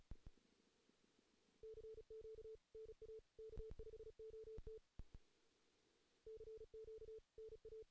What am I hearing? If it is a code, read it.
Morse code, heard as a keyed tone, about 37 wpm, 440 Hz. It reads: CQ NA K5OT CQ NA
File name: 2016 JAN CW NAQP / 2016 Jan naqp k5ot - 0-3